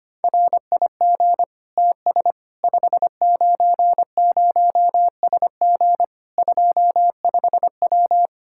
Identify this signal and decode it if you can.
Morse code, heard as a keyed tone, about 25 wpm, 700 Hz. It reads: RIG TS 590SG 25W
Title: Practice QSO - Exchange 2 - Info - 25wpm